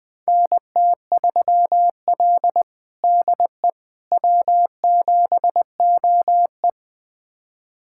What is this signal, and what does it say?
Morse code, heard as a keyed tone, about 21 wpm, 700 Hz. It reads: NT3L DE W7OE
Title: USA Callsign DE Callsign - 4 Characters - 20wpm